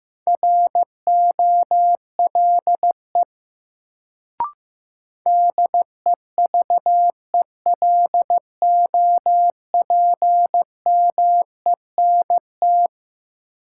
Morse code, heard as a keyed tone, about 15 wpm, 700 Hz. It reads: ROLE DEVELOPMENT
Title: Sets of 3 Words - New Word Drill - 15wpm